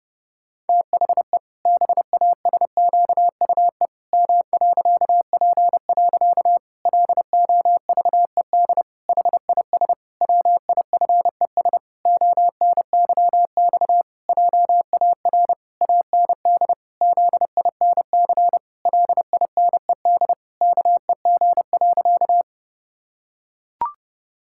Morse code, heard as a keyed tone, about 32 wpm, 700 Hz. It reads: THE BASQUE M.P. LOVED HIS WIFES ONYX JAR AND ZINC LINED KEG.
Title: Pangram Challenge v2 - 30wpm